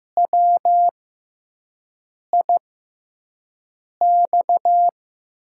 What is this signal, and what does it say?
Morse code, heard as a keyed tone, about 15 wpm, spaced as for 5 wpm, 700 Hz. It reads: WIX
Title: USA Call Signs - 6x Character Spacing - 15wpm